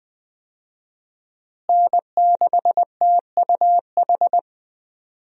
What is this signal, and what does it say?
Morse code, heard as a keyed tone, about 20 wpm, 700 Hz. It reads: N6TUH
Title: USA Call Signs - 5 Characters - Repeated 3x - 20wpm